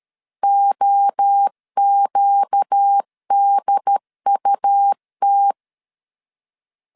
Morse code, heard as a keyed tone, about 13 wpm, 800 Hz. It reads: OQDUT
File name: CW / DET / 1 6 1500C S A 12wpm 3440